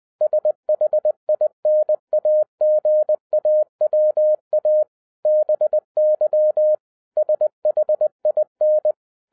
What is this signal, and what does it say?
Morse code, heard as a keyed tone, about 20 wpm, 610 Hz. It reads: SHINAGAWA BY SHIN